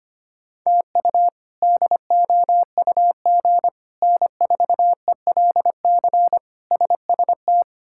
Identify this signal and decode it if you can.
Morse code, heard as a keyed tone, about 25 wpm, 700 Hz. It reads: TU DOUG N4ELC SST
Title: K1USN SST QSO Practice - 25wpm